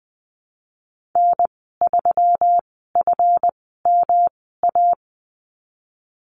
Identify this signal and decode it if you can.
Morse code, heard as a keyed tone, about 19 wpm, spaced as for 14 wpm, 700 Hz. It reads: N3FMA
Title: 147 Lesson 24g - Letter y - Call Signs 20wpm